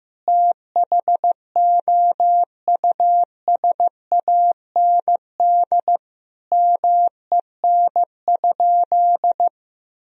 Morse code, heard as a keyed tone, about 16 wpm, 700 Hz. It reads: THOUSAND MEN?
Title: Sentences from Top 1000 Words - Review Part 3 - 15wpm